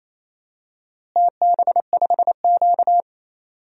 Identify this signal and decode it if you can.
Morse code, heard as a keyed tone, about 28 wpm, 700 Hz. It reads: TB5Q
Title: DXCC-100 Call Signs - 4 Characters - Repeated 3x - 28wpm